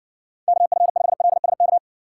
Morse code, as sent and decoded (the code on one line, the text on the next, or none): -.- ..-. ..... .-.. ... -.-
KF5LSK